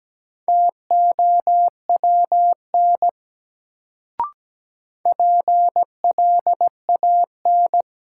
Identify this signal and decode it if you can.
Morse code, heard as a keyed tone, about 17 wpm, 700 Hz. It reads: TOWN PLAN
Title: Sentences from Top 500 Words - New Word Drill - 17wpm